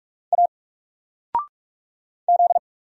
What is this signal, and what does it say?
Morse code, heard as a keyed tone, about 44 wpm, 700 Hz. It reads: A Z
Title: Single Letter-Number 45wpm